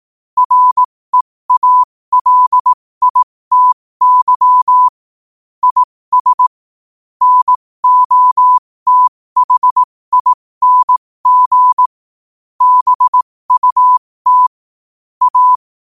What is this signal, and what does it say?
Morse code, heard as a keyed tone, about 18 wpm, spaced as for 15 wpm, 1000 Hz. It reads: REALITY IS NOTHING BUT A
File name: qotdfiles / 15wpm